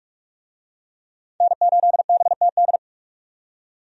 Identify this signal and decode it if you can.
Morse code, heard as a keyed tone, about 44 wpm, 700 Hz. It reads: N8BTD
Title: USA Call Signs - 5 Characters - 45wpm